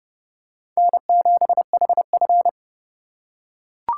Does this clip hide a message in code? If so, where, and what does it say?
Yes: Morse code from 0.8 to 2.5 s, heard as a keyed tone, about 29 wpm, 700 Hz. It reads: N7HF